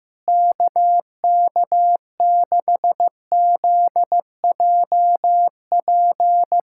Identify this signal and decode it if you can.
Morse code, heard as a keyed tone, about 16 wpm, 700 Hz. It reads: KK6ZJP
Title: USA Callsign DE Callsign - 6 Characters - 15wpm